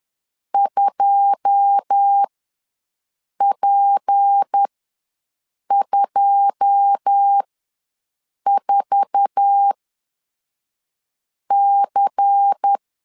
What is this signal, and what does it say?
Morse code, heard as a keyed tone, about 10 wpm, spaced as for 5 wpm, 800 Hz. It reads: 2P24C
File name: LEZIONI / 9 2 ws 10wpm 4130